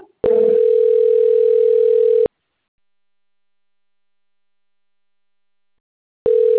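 Ringback tone (North America), 440 + 480 Hz.
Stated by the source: Outgoing phone call recording, Americana 1.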